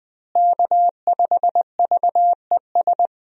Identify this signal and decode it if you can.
Morse code, heard as a keyed tone, about 20 wpm, 700 Hz. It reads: K5VES
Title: USA Call Signs - 5 Characters - Repeated 3x - 20wpm